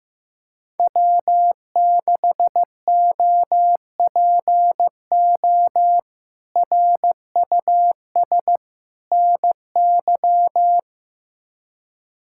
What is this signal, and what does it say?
Morse code, heard as a keyed tone, about 15 wpm, 700 Hz. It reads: W6OPO RUS NY
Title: K1USN SST QSO Practice - 15wpm